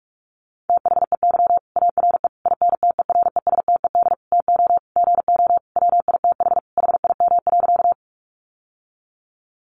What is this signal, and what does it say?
Morse code, heard as a keyed tone, about 44 wpm, 700 Hz. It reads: THEY ARE INTERESTED TO GO WITH HIM.
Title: Sentences from Top 900 Words 45wpm